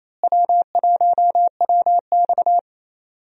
Morse code, heard as a keyed tone, about 28 wpm, 700 Hz. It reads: W1WX
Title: USA Call Signs - 4 Characters - Repeated 3x - 28wpm